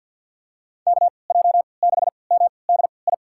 Morse code, heard as a keyed tone, about 47 wpm, spaced as for 28 wpm, 700 Hz. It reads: KJ6MDI
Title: USA Call Signs - 6 Characters - 3x Character Spacing - 50wpm